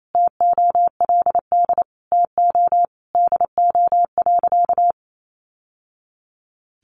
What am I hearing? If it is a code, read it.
Morse code, heard as a keyed tone, about 28 wpm, 700 Hz. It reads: TOLD TO DO.
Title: Sentences from Top 1100 Words 28wpm